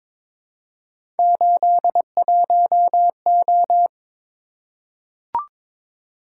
Morse code, heard as a keyed tone, about 22 wpm, 700 Hz. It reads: 81O
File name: practice / 22wpm / Three Letter-Number Combinations - 22wpm